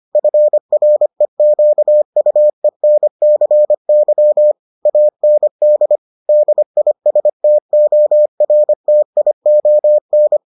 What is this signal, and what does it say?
Morse code, heard as a keyed tone, about 25 wpm, 600 Hz. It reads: FREQUENCY AND DISTORTION